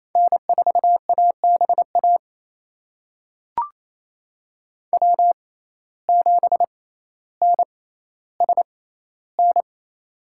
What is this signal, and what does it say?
Morse code, heard as a keyed tone, about 28 wpm, spaced as for 14 wpm, 700 Hz. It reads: N4ABA W7NSN
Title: USA Call Signs - 5 Characters - 6x Character Spacing - 28wpm